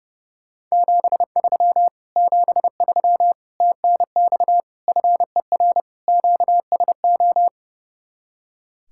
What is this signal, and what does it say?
Morse code, heard as a keyed tone, about 38 wpm, 700 Hz. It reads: 73 73 TNX FER QSO